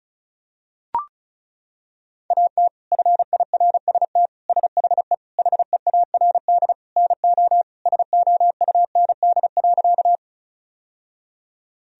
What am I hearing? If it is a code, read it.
Morse code, heard as a keyed tone, about 34 wpm, 700 Hz. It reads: AT FIRST SHE HEARD NO SOUND.
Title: Sentences from Top 1000 Words - Review Part 1 - 35wpm